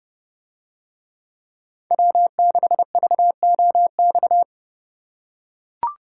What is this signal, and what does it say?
Morse code, heard as a keyed tone, about 29 wpm, 700 Hz. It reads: W6VOX